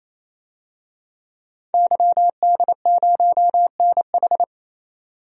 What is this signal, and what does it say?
Morse code, heard as a keyed tone, about 28 wpm, 700 Hz. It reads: YD0NH
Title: DXCC-100 Call Signs - 5 Characters - 28wpm